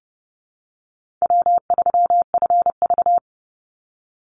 Morse code, heard as a keyed tone, about 30 wpm, 700 Hz. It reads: W3FV
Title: Easy Call Signs - 30wpm